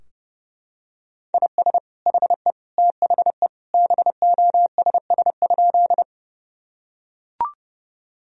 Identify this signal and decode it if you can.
Morse code, heard as a keyed tone, about 30 wpm, 700 Hz. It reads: IS HE THE BOSS?
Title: Sentences from Top 1000 Words - Review Part 4 - 30wpm